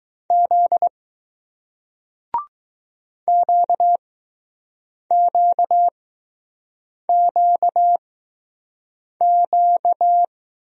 Morse code, heard as a keyed tone, about 20 wpm, spaced as for 6 wpm, 700 Hz. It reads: Z QQQQ